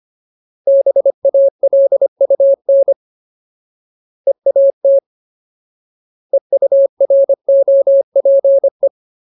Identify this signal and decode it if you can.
Morse code, heard as a keyed tone, about 25 wpm, 550 Hz. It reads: BALUN EAT EUROPE
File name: Ultimate Word List / 25WPM Ultimate